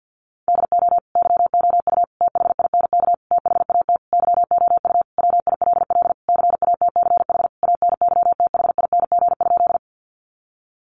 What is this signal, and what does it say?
Morse code, heard as a keyed tone, about 50 wpm, 700 Hz. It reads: DO YOU THINK THAT YOU WILL CATCH ANYTHING?
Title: Sentences from Top 1500 Words 50wpm